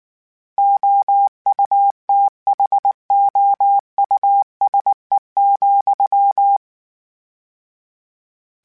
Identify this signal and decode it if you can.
Morse code, heard as a keyed tone, about 19 wpm, 800 Hz. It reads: OUTHOUSE,
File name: General / General-mp3s / Houses @18